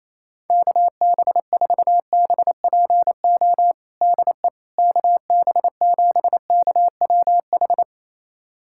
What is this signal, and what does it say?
Morse code, heard as a keyed tone, about 28 wpm, 700 Hz. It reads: KB4BPO DE KB7KWH
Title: USA Callsign DE Callsign - 6 Characters - 28wpm